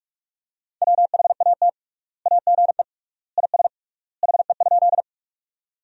Morse code, heard as a keyed tone, about 46 wpm, 700 Hz. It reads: WHAT AGE IS HE?
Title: 3x Word-Spacing - Sentences from Top 500 Words - 45wpm